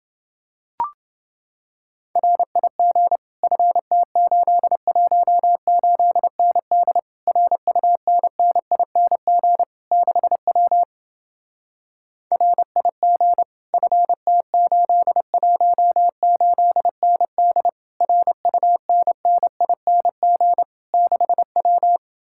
Morse code, heard as a keyed tone, about 29 wpm, 700 Hz. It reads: RIG FT818ND RUNNING 6W RIG FT818ND RUNNING 6W